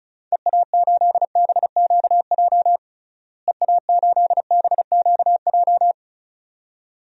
Morse code, heard as a keyed tone, about 34 wpm, 700 Hz. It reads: EA8BQJ EA8BQJ
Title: DXCC-100 Call Signs - 6 Characters - Repeated 2x - 35wpm